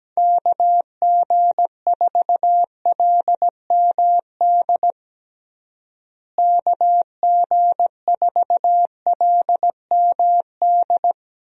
Morse code, heard as a keyed tone, about 17 wpm, 700 Hz. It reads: KG4LMD KG4LMD
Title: USA Call Signs - 6 Characters - Repeated 2x - 17wpm